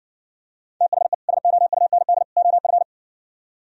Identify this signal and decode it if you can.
Morse code, heard as a keyed tone, about 59 wpm, 700 Hz. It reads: THE SOUND OF